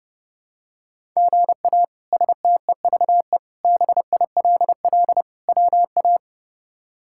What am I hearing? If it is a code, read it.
Morse code, heard as a keyed tone, about 30 wpm, 700 Hz. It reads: GA STEVE BILL WA
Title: K1USN SST QSO Practice - 30wpm